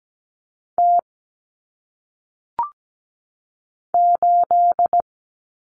Morse code, heard as a keyed tone, about 17 wpm, 700 Hz. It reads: T 8
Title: Common Cut Numbers - 1-Digit 17wpm